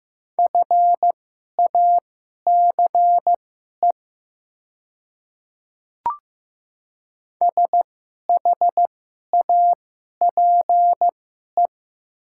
Morse code, heard as a keyed tone, about 15 wpm, spaced as for 11 wpm, 700 Hz. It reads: FACE SHAPE
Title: 2x Character Spacing - Top 500 Words - 15wpm